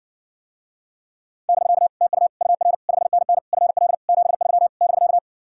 Morse code, heard as a keyed tone, about 60 wpm, 700 Hz. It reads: <BK> TU UR 5NN FL 73 <BK>